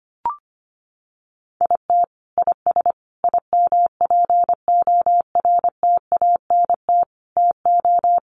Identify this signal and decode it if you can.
Morse code, heard as a keyed tone, about 25 wpm, 700 Hz. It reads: IT IS IMPORTANT TO
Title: Sets of 5 Words 25wpm